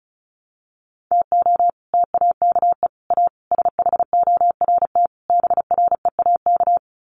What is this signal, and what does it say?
Morse code, heard as a keyed tone, about 34 wpm, 700 Hz. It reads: TO TAKE A SHORT BREAK
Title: Sets of 5 Words Encore2 35wpm